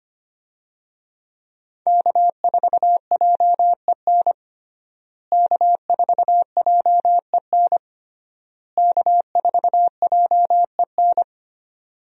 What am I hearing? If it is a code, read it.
Morse code, heard as a keyed tone, about 25 wpm, 700 Hz. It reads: K4JEN K4JEN K4JEN
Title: USA Call Signs - 5 Characters - Repeated 3x - 25wpm